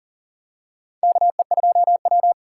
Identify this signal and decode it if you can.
Morse code, heard as a keyed tone, about 40 wpm, 700 Hz. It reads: KE2W
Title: USA Call Signs - 3x Character Spacing - 40wpm